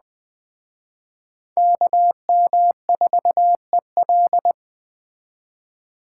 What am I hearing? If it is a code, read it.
Morse code, heard as a keyed tone, about 20 wpm, 700 Hz. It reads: KM4EL
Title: Speed Racing - USA 2x2 Call Signs 17wpm